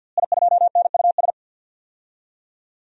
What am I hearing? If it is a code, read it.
Morse code, heard as a keyed tone, about 52 wpm, 700 Hz. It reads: I2NUS
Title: DXCC-100 Call Signs - 5 Characters - Repeated 2x - 50wpm